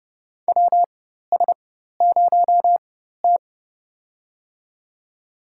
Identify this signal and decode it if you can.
Morse code, heard as a keyed tone, about 30 wpm, spaced as for 14 wpm, 700 Hz. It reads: WS0T
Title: USA Call Signs - 4 Characters - 4x Character Spacing - 30wpm